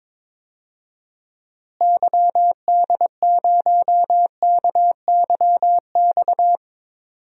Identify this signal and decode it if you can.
Morse code, heard as a keyed tone, about 22 wpm, 700 Hz. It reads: YD0KYX
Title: DXCC-100 Call Signs - 6 Characters - Repeated 2x - 22wpm